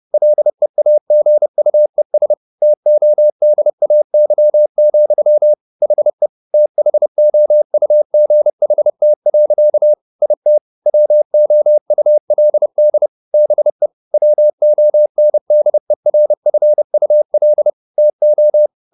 Morse code, heard as a keyed tone, about 30 wpm, 600 Hz. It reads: LEAGUES TODAY, HE THOUGHT. IT WOULD BE WONDERFUL TO
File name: the old man and the sea / the old man and the sea 1 30wpm